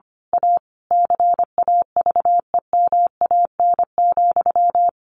Morse code, heard as a keyed tone, about 25 wpm, 700 Hz. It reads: A CAVEMAN,
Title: A Sound Of Thunder 25wpm